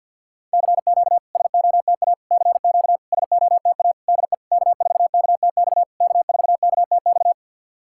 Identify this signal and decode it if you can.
Morse code, heard as a keyed tone, about 52 wpm, 690 Hz. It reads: CQ SOTA CQ SOTA DE K4KTX K4KTX
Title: SOTA QSO Practice 50wpm